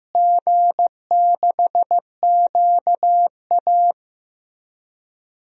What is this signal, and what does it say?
Morse code, heard as a keyed tone, about 15 wpm, 700 Hz. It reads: G6QA
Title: DXCC-100 Call Signs - 4 Characters - Repeated 3x - 15wpm